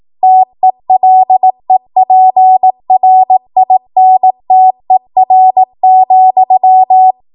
Morse code, heard as a keyed tone, about 18 wpm, 750 Hz. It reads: TELEPRINTER,
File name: Archive / 18 WPM / 18WPM